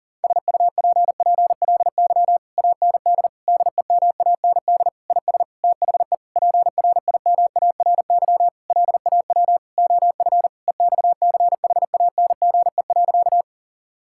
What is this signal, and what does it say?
Morse code, heard as a keyed tone, about 40 wpm, 700 Hz. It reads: SUPPLY AND DEMAND IS THE PRIMARY LAW OF EXCHANGE.
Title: Sentences from Top 1000 Words - Review Part 6 - 40wpm